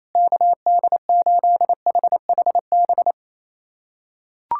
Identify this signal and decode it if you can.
Morse code, heard as a keyed tone, about 28 wpm, 700 Hz. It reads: KD8HHB
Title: USA Callsign DE Callsign - 6 Characters - 28wpm